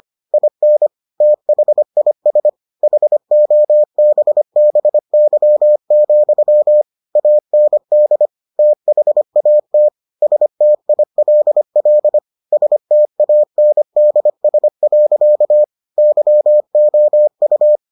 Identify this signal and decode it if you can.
Morse code, heard as a keyed tone, about 25 wpm, 600 Hz. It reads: IN THIS HOBBY, AND THAT STILL STANDS. YOU